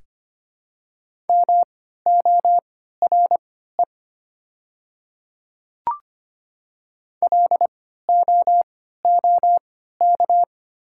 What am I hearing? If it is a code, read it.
Morse code, heard as a keyed tone, about 25 wpm, spaced as for 14 wpm, 690 Hz. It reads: MORE LOOK
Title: 3x Character Spacing - Top 100 Words - 25wpm